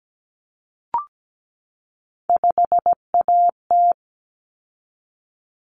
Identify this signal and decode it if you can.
Morse code, heard as a keyed tone, about 17 wpm, 700 Hz. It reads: 5AT
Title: Common Cut Numbers - 3-Digit 17wpm